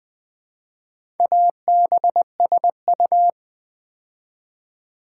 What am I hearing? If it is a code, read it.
Morse code, heard as a keyed tone, about 20 wpm, 700 Hz. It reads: ABSU
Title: Build A Word - 2 Letters At A Time - 6-Letter Words - 20wpm